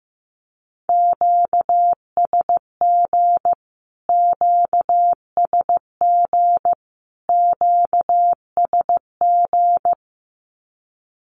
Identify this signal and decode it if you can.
Morse code, heard as a keyed tone, about 15 wpm, 700 Hz. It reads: QSG QSG QSG
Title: Q-Codes Repeated 3x 15wpm